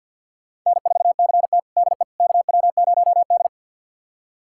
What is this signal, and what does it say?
Morse code, heard as a keyed tone, about 46 wpm, 700 Hz. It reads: N4CT DE KW0D
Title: USA Callsign DE Callsign - 4 Characters - 50wpm